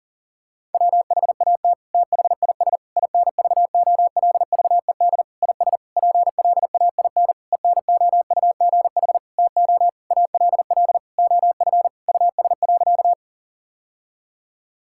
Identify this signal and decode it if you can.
Morse code, heard as a keyed tone, about 40 wpm, 700 Hz. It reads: WHAT THIS INVOLVED IS PLAIN ENOUGH TO ALL OF US.
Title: Sentences from Top 1000 Words - Review Part 5 - 40wpm